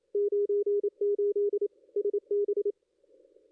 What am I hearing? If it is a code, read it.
Morse code, heard as a keyed tone, about 28 wpm, 410 Hz. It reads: 98 SB
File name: contests / ss-cw-k5tr-op-n5rz -2200